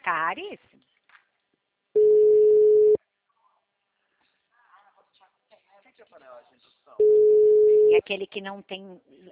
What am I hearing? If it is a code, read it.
Ringback tone (Europe, 425 Hz), with speech.